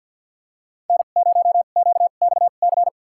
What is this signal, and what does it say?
Morse code, heard as a keyed tone, about 52 wpm, spaced as for 37 wpm, 700 Hz. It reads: N0QXC
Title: USA Call Signs - 5 Characters - 2x Character Spacing - 50wpm